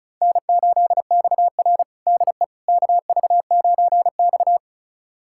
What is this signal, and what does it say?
Morse code, heard as a keyed tone, about 36 wpm, 700 Hz. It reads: N8XR DE KV9X
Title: USA Callsign DE Callsign - 4 Characters - 35wpm